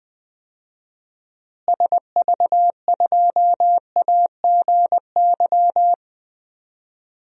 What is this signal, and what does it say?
Morse code, heard as a keyed tone, about 20 wpm, 700 Hz. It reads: SV2AGY
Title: DXCC-100 Call Signs - 6 Characters - Repeated 2x - 20wpm